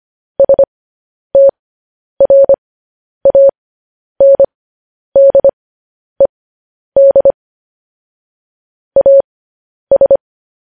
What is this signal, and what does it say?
Morse code, heard as a keyed tone, about 25 wpm, spaced as for 10 wpm, 550 Hz. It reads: STRANDED AS
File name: podcast / News headlines-1x-25x10